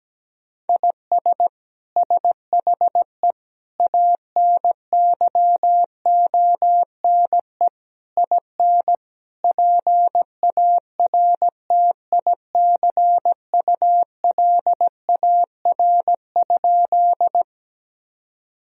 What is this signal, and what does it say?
Morse code, heard as a keyed tone, about 17 wpm, 700 Hz. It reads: IS SHE ANYONE IN PARTICULAR?
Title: Sentences from Top 1000 Words - Review Part 4 - 17wpm